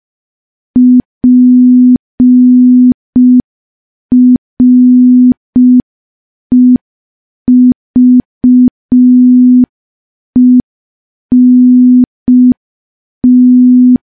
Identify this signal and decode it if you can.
Morse code, heard as a keyed tone, about 5 wpm, 250 Hz. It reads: PREVENT
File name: morse / morse-training-05WPM